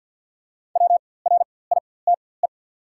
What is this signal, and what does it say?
Morse code, heard as a keyed tone, about 48 wpm, spaced as for 23 wpm, 690 Hz. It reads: WRITE
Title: 4x Character Spacing - Top 100 Words - 50wpm